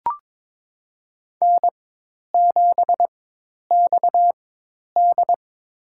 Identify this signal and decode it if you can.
Morse code, heard as a keyed tone, about 22 wpm, spaced as for 10 wpm, 700 Hz. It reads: N7XD
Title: USA Call Signs - 4 Characters - 4x Character Spacing - 22wpm